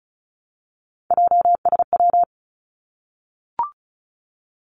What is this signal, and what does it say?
Morse code, heard as a keyed tone, about 36 wpm, 700 Hz. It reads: JSW
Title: Three Challenging Letter-Number Combinations 35wpm